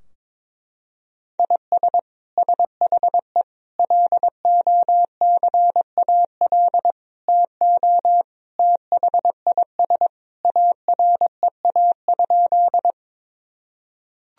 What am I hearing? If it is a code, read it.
Morse code, heard as a keyed tone, about 22 wpm, 700 Hz. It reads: IS SHE LOCAL TO THIS AREA?
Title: Sentences from Top 1000 Words - Review Part 3 - 22wpm